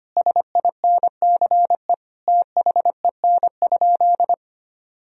Morse code, heard as a keyed tone, about 25 wpm, 700 Hz. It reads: SINCE THEN?
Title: Sentences from Top 1000 Words - Review Part 1 - 25wpm